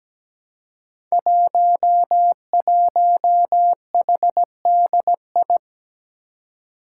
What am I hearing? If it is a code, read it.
Morse code, heard as a keyed tone, about 17 wpm, 700 Hz. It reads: 11HDI
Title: Five Challenging Letter-Number Combinations 17wpm